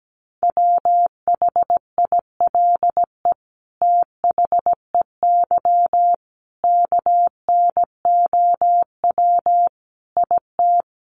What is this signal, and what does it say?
Morse code, heard as a keyed tone, about 17 wpm, 700 Hz. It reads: WHILE THEY KNOW IT
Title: Sentences from Top 1400 Words 17wpm